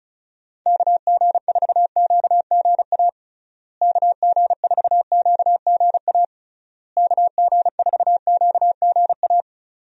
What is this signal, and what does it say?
Morse code, heard as a keyed tone, about 34 wpm, 700 Hz. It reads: KG4QGA KG4QGA KG4QGA
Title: USA Call Signs - 6 Characters - Repeated 3x - 35wpm